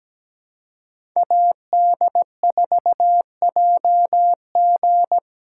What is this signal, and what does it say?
Morse code, heard as a keyed tone, about 20 wpm, 700 Hz. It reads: AD4JG
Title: Speed Racing - USA 2x2 Call Signs 17wpm